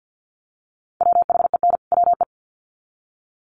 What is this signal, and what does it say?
Morse code, heard as a keyed tone, about 50 wpm, 700 Hz. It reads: WHEN WE
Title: Sets of 2 Words 50wpm